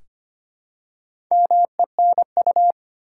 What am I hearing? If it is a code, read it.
Morse code, heard as a keyed tone, about 25 wpm, 700 Hz. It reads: MENU
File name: practice / 25wpm / Easy Words with Hints - 25wpm